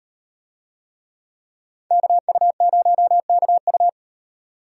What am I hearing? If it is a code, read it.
Morse code, heard as a keyed tone, about 38 wpm, 700 Hz. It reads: KU0KU